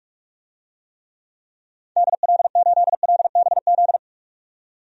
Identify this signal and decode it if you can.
Morse code, heard as a keyed tone, about 47 wpm, 700 Hz. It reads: DL8LBZ
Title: DXCC-100 Call Signs - 6 Characters - 45wpm